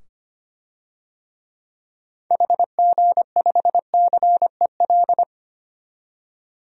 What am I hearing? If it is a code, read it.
Morse code, heard as a keyed tone, about 25 wpm, 700 Hz. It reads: HG5CEL
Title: DXCC-100 Call Signs - 6 Characters - 25wpm